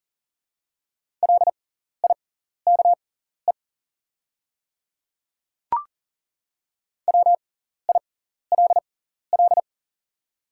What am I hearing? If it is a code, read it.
Morse code, heard as a keyed tone, about 40 wpm, spaced as for 14 wpm, 700 Hz. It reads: LIKE WILL
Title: 6x Character Spacing - Top 100 Words - 40wpm